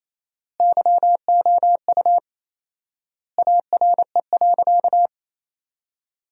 Morse code, heard as a keyed tone, about 28 wpm, 700 Hz. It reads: YOU ARE.
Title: 4x Word-Spacing - Sentences from Top 500 Words - 28wpm